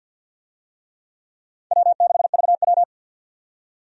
Morse code, heard as a keyed tone, about 51 wpm, 700 Hz. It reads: W6VW
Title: USA Call Signs - 4 Characters - 50wpm